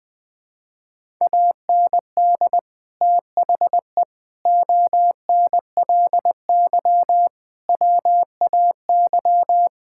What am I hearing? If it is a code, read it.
Morse code, heard as a keyed tone, about 20 wpm, 700 Hz. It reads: AND THE ONLY WAY
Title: Sentences with 9-Word Phrases - Spoken Beginning - 20wpm